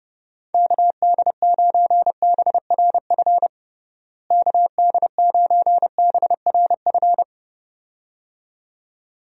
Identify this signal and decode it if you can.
Morse code, heard as a keyed tone, about 32 wpm, 700 Hz. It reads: KD9BRF KD9BRF
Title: USA Call Signs - 6 Characters - Repeated 2x - 30wpm